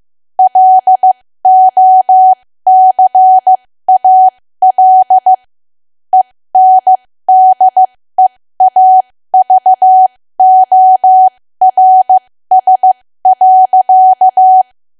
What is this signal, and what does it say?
Morse code, heard as a keyed tone, about 15 wpm, spaced as for 13 wpm, 750 Hz. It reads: LOCAL ENDEAVORS.
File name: cwa / W1AW-Level2 / 13WPM